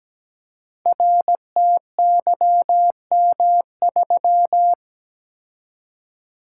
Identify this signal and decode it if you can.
Morse code, heard as a keyed tone, about 17 wpm, 700 Hz. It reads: RTYM3
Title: Code Groups - 5 Letter:Numbers - 17wpm